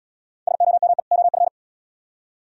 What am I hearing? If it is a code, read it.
Morse code, heard as a keyed tone, about 73 wpm, 700 Hz. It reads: SOME OF